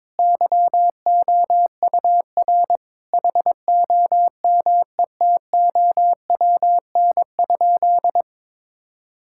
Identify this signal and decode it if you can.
Morse code, heard as a keyed tone, about 22 wpm, 700 Hz. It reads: YOUR HOMETOWN?
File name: practice / 22wpm / Sentences with 8-Word Phrases - 22wpm